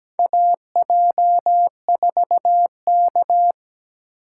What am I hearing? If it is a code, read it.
Morse code, heard as a keyed tone, about 17 wpm, 700 Hz. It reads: AJ4K
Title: USA Call Signs - 4 Characters - 4x Character Spacing - 17wpm